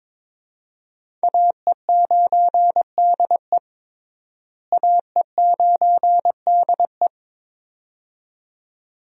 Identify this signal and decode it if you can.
Morse code, heard as a keyed tone, about 21 wpm, 700 Hz. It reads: AE9DE AE9DE
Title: USA Call Signs - 5 Characters - Repeated 2x - 22wpm